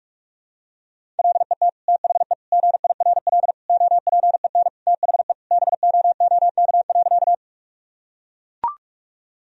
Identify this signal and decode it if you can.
Morse code, heard as a keyed tone, about 42 wpm, 700 Hz. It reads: LET THE GIRL OPEN THE BOOK.
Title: Sentences from Top 1000 Words - Review Part 1 - 45wpm